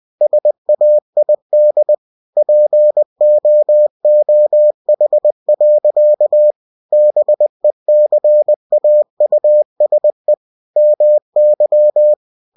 Morse code, heard as a keyed tone, about 20 wpm, 600 Hz. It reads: SAID POOH. BECAUSE MY
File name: cw-books / winnie-the-pooh / winnie-20wpm-page-0051